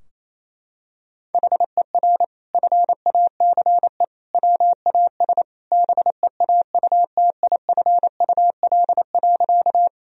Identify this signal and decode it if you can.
Morse code, heard as a keyed tone, about 28 wpm, 700 Hz. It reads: HER FACE WAS BEAUTIFUL.
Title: Sentences from Top 1000 Words - Review Part 2 - 28wpm